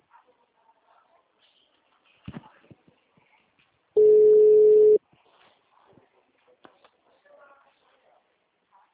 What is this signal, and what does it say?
Ringback tone (Europe, 425 Hz).
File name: out- -2011- - - .55257